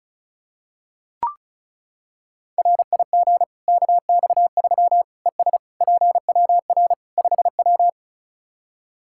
Morse code, heard as a keyed tone, about 36 wpm, 700 Hz. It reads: RIG KX3 ES PWR 5W
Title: Practice QSO - Exchange 2 - Info - 35wpm